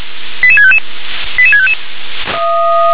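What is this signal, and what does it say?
Five-tone selective call (ZVEI), heard as quick runs of five tones, tones of about 70 ms. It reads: 88655 88655
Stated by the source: Five-tone ZVEI selective call received on an SDR.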